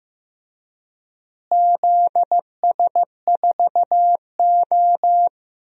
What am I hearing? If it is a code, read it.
Morse code, heard as a keyed tone, about 16 wpm, 700 Hz. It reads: ZS4O
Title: DXCC-100 Call Signs - 4 Characters - Repeated 3x - 15wpm